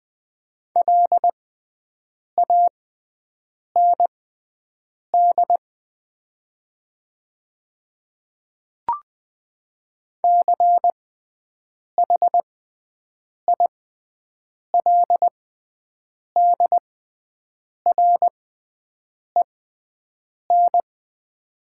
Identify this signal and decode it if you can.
Morse code, heard as a keyed tone, about 20 wpm, spaced as for 7 wpm, 700 Hz. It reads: LAND CHILDREN
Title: 6x Character Spacing - Top 300 Words - 20wpm